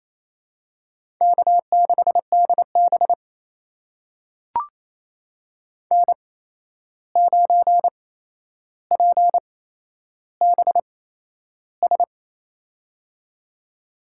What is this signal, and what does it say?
Morse code, heard as a keyed tone, about 29 wpm, spaced as for 11 wpm, 700 Hz. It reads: K6DB N9PBS
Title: USA Call Signs - 8x Character Spacing - 28wpm